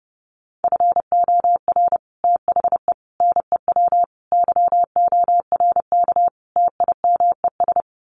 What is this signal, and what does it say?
Morse code, heard as a keyed tone, about 30 wpm, 700 Hz. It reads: FOR THE NEW YORK TIMES
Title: Sets of 5 Words 30wpm